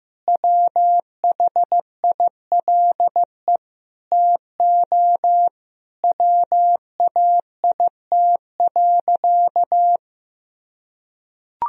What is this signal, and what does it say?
Morse code, heard as a keyed tone, about 15 wpm, 700 Hz. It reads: WHILE TO WAIT.
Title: Sentences from Top 1000 Words - Review Part 1 - 15wpm